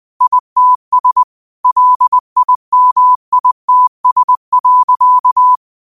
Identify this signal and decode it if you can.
Morse code, heard as a keyed tone, about 20 wpm, 1000 Hz. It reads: ITS LIMITS.
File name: qotdfiles / 20wpm